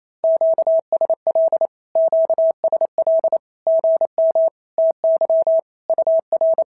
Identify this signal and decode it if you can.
Morse code, heard as a keyed tone, about 27 wpm, 650 Hz. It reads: QSL QSL GM TY UR
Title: SOTA QSO Practice 28wpm